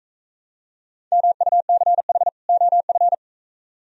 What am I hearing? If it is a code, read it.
Morse code, heard as a keyed tone, about 45 wpm, 700 Hz. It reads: MUCH OF